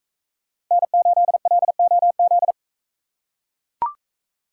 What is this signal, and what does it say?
Morse code, heard as a keyed tone, about 40 wpm, 700 Hz. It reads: N8LOZ